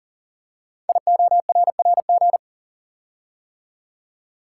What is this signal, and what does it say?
Morse code, heard as a keyed tone, about 40 wpm, 700 Hz. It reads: IORRG
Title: Code Groups - 5 Letters - 40wpm